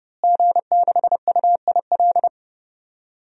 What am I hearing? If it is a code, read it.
Morse code, heard as a keyed tone, about 32 wpm, 700 Hz. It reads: G6UIL
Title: DXCC-100 Call Signs - 5 Characters - Repeated 3x - 30wpm